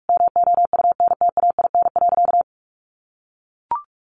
Morse code, heard as a keyed tone, about 44 wpm, 700 Hz. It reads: MOUNTAIN.
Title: Sentences from Top 1300 Words 45wpm